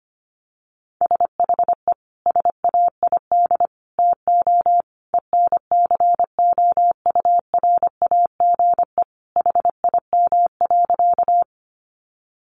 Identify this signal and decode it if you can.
Morse code, heard as a keyed tone, about 25 wpm, 700 Hz. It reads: SHE SAID TO ENCOURAGE HIM.
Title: Sentences from Top 1400 Words 25wpm